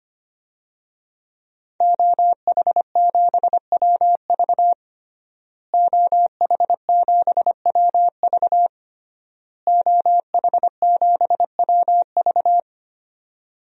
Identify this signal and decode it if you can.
Morse code, heard as a keyed tone, about 25 wpm, 700 Hz. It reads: OH7WV OH7WV OH7WV
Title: DXCC-100 Call Signs - 5 Characters - Repeated 3x - 25wpm